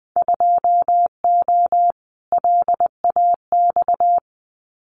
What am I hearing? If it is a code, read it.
Morse code, heard as a keyed tone, about 20 wpm, 700 Hz. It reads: 2O LAX
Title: Winter Field Day Exchanges 20wpm